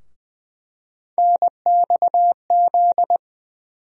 Morse code, heard as a keyed tone, about 20 wpm, 700 Hz. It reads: NXZ
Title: USA Call Sign Postfixes - 3 Letters 20wpm